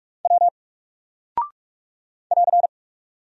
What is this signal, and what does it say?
Morse code, heard as a keyed tone, about 44 wpm, 700 Hz. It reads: W <AR>
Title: Single Character (Letter:Number:Punctuation:Prosign) 45wpm